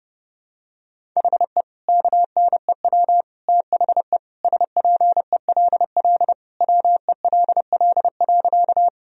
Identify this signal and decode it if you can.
Morse code, heard as a keyed tone, about 30 wpm, 700 Hz. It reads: HE KNEW THE SPELL WELL.
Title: Sentences from Top 1000 Words - Review Part 1 - 30wpm